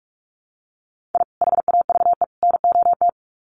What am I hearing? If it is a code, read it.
Morse code, heard as a keyed tone, about 44 wpm, 700 Hz. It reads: I HAVE NOT
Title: Sets of 3 Words 45wpm